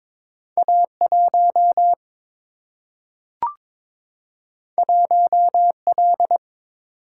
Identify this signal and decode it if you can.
Morse code, heard as a keyed tone, about 22 wpm, 700 Hz. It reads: A 1 1 L
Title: Single-Letter Single-Number 22wpm